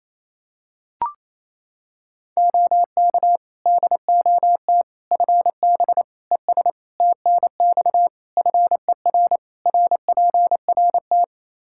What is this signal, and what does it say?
Morse code, heard as a keyed tone, about 27 wpm, 700 Hz. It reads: OK DOT FB ES TNX FER RPRT
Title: Practice QSO - Exchange 2 - Info - 28wpm